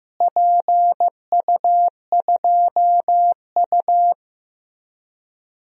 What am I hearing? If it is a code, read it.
Morse code, heard as a keyed tone, about 15 wpm, 700 Hz. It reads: PU2U
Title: DXCC-100 Call Signs - 4 Characters - Repeated 3x - 15wpm